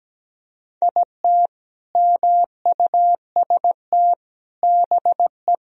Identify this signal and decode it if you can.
Morse code, heard as a keyed tone, about 17 wpm, 700 Hz. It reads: IT MUST BE
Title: Sentences from Top 1000 Words - Review Part 1 - 17wpm